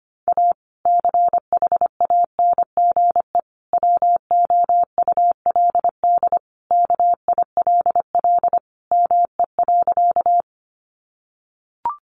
Morse code, heard as a keyed tone, about 25 wpm, 700 Hz. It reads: A CHANGE WOULD KILL ME.
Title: Sentences from Top 1400 Words 25wpm